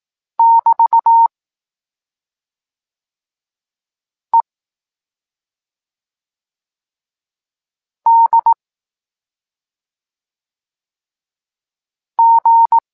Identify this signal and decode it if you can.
Morse code, heard as a keyed tone, about 18 wpm, 920 Hz. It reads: <BT> E D G
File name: Tekst-23 4